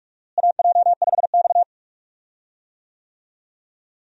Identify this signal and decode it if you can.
Morse code, heard as a keyed tone, about 46 wpm, 700 Hz. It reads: AJ5X
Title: USA Call Signs - 4 Characters - Repeated 2x - 45wpm